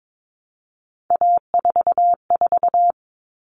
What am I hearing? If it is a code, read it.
Morse code, heard as a keyed tone, about 22 wpm, 700 Hz. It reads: A44
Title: All Cut Numbers - 3-Digit 22wpm